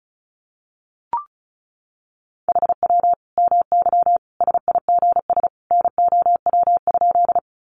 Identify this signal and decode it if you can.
Morse code, heard as a keyed tone, about 34 wpm, 700 Hz. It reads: HW MY SIGS NOW?
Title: Common QSO Phrases 35wpm